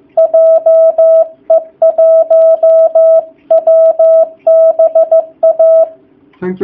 Morse code, heard as a keyed tone, about 15 wpm, 660 Hz. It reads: JE1WBA
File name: jh8cbh / 2019cw / sp15- w